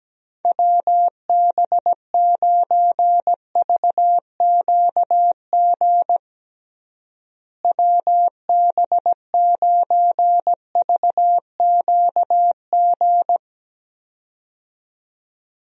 Morse code, heard as a keyed tone, about 17 wpm, 700 Hz. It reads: WB9VQG WB9VQG
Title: USA Call Signs - 6 Characters - Repeated 2x - 17wpm